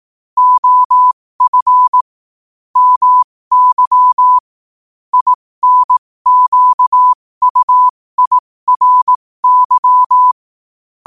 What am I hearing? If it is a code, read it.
Morse code, heard as a keyed tone, about 18 wpm, spaced as for 15 wpm, 1000 Hz. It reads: OF MY INQUIRY
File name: qotdfiles / 15wpm